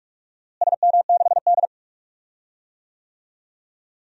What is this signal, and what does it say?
Morse code, heard as a keyed tone, about 47 wpm, 700 Hz. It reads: SM6D
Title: DXCC-100 Call Signs - 4 Characters - Repeated 2x - 45wpm